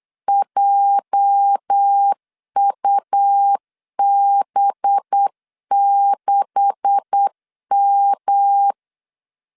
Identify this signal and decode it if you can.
Morse code, heard as a keyed tone, about 8 wpm, 800 Hz. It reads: JUB6M
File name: CW / DET / 1 9 1500C S B 8wpm 5624